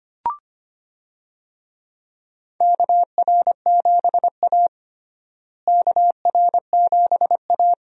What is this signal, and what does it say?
Morse code, heard as a keyed tone, about 25 wpm, 700 Hz. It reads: KR7A KR7A
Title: USA Call Signs - 4 Characters - Repeated 2x - 25wpm